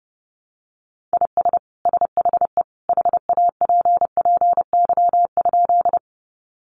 Morse code, heard as a keyed tone, about 30 wpm, 700 Hz. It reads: IS SHE HAPPY?
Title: Sentences from Top 1000 Words 30wpm